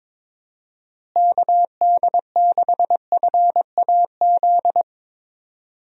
Morse code, heard as a keyed tone, about 22 wpm, 700 Hz. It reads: KD6FAZ
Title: USA Call Signs - 6 Characters - Repeated 2x - 22wpm